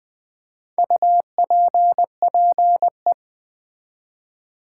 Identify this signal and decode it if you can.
Morse code, heard as a keyed tone, about 20 wpm, 700 Hz. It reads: UPPE
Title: Build A Word - 1 Letter At A Time - 5-Letter Words - 20wpm